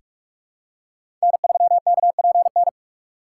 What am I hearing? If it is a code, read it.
Morse code, heard as a keyed tone, about 46 wpm, 700 Hz. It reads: N3KPN
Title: USA Call Signs - 5 Characters - 6x Character Spacing - 45wpm